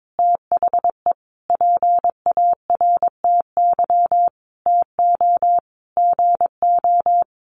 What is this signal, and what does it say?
Morse code, heard as a keyed tone, about 22 wpm, 700 Hz. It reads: THE PARTY TO GO
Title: Sentences from Top 1400 Words 22wpm